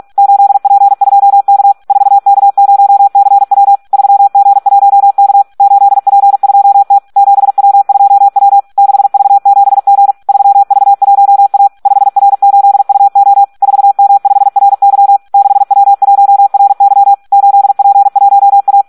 Morse code, heard as a keyed tone, about 46 wpm, 800 Hz. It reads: 8P2K 4K0CW 3G1K 8P2T 7W2W 6V7G 3V1A 5R8UO 4M5RY 6W1RY 8P1A
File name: Audio practica CW / SD 1 46